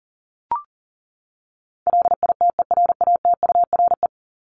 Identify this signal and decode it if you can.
Morse code, heard as a keyed tone, about 40 wpm, 700 Hz. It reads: LITERATURE
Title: Top 501-750 Words 40wpm